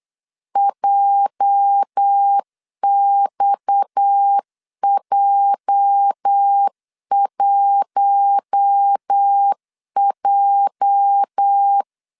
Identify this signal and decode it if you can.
Morse code, heard as a keyed tone, about 8 wpm, 800 Hz. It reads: JXJ1J
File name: LEZIONI / 8 YJ45 / 8 YJ 8wpm 3313